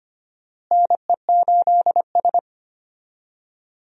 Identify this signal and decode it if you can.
Morse code, heard as a keyed tone, about 25 wpm, 700 Hz. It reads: NE8S
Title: USA Call Signs - 4 Characters - Repeated 3x - 25wpm